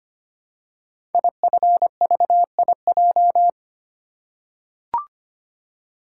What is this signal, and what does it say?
Morse code, heard as a keyed tone, about 25 wpm, 700 Hz. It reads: IFVIJ
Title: Code Groups - 5 Letters - 25wpm